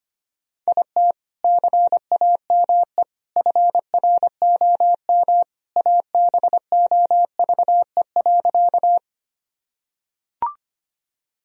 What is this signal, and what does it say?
Morse code, heard as a keyed tone, about 25 wpm, 700 Hz. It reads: IT CAME FROM ABOVE.
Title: Sentences from Top 1000 Words - Review Part 2 - 25wpm